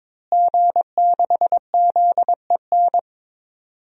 Morse code, heard as a keyed tone, about 22 wpm, 700 Hz. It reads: G6ZEN
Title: DXCC-100 Call Signs - 5 Characters - Repeated 3x - 22wpm